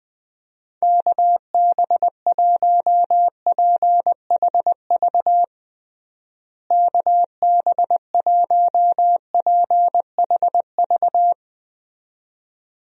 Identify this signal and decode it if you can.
Morse code, heard as a keyed tone, about 21 wpm, 700 Hz. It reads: KB1PHV KB1PHV
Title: USA Call Signs - 6 Characters - Repeated 2x - 20wpm